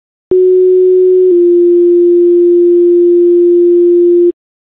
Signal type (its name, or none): two-tone page